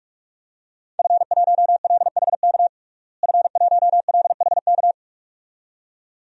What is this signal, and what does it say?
Morse code, heard as a keyed tone, about 46 wpm, 700 Hz. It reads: F1LHK F1LHK
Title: DXCC-100 Call Signs - 5 Characters - Repeated 2x - 45wpm